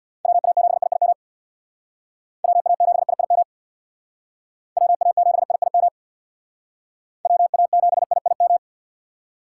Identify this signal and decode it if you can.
Morse code, heard as a keyed tone, about 63 wpm, 700 Hz. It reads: WA7IIM WA7IIM WA7IIM WA7IIM